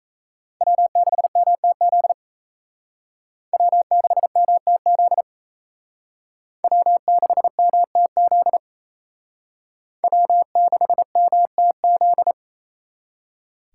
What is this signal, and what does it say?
Morse code, heard as a keyed tone, about 35 wpm, 700 Hz. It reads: W6MTZ W6MTZ W6MTZ W6MTZ
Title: Speed Racing - USA 1x3 Call Signs 28wpm